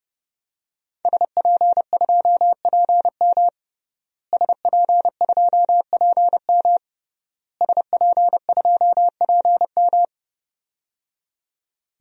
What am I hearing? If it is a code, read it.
Morse code, heard as a keyed tone, about 30 wpm, 700 Hz. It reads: SP2PM SP2PM SP2PM
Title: DXCC-100 Call Signs - 5 Characters - Repeated 3x - 30wpm